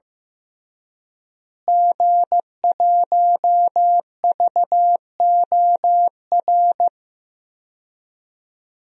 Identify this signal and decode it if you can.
Morse code, heard as a keyed tone, about 15 wpm, 700 Hz. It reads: G1VOR
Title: DXCC-100 Call Signs - 5 Characters - Repeated 2x - 15wpm